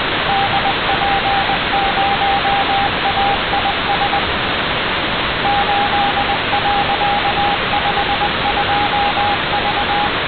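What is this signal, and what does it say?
Morse code, heard as a keyed tone, about 20 wpm, 800 Hz, with radio noise. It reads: DP0AIS 8.52V
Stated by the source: Satellite Morse downlink received on the ground; no further detail about this signal.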